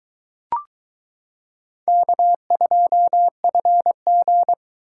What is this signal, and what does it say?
Morse code, heard as a keyed tone, about 21 wpm, 700 Hz. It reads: K2FG